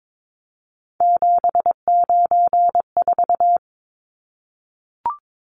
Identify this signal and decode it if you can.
Morse code, heard as a keyed tone, about 22 wpm, 700 Hz. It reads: 794
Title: Numbers - 3-Digit 22wpm